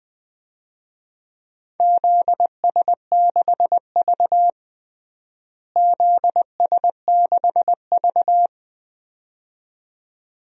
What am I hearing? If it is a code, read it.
Morse code, heard as a keyed tone, about 21 wpm, 700 Hz. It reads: ZS6V ZS6V
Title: DXCC-100 Call Signs - 4 Characters - Repeated 2x - 20wpm